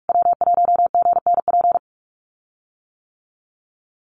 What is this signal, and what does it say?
Morse code, heard as a keyed tone, about 44 wpm, 690 Hz. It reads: W1GNP
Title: USA Call Signs Repeated 3x 45wpm